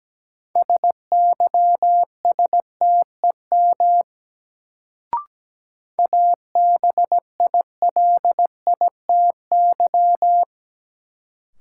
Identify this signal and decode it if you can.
Morse code, heard as a keyed tone, about 17 wpm, 700 Hz. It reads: SYSTEM ABILITY
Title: Sentences from Top 1000 Words - Review Part 3 - New Word Drill - 17wpm